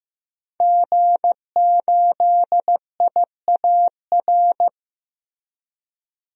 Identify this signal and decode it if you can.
Morse code, heard as a keyed tone, about 16 wpm, 700 Hz. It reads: G8IAR
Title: DXCC-100 Call Signs - 5 Characters - Repeated 2x - 15wpm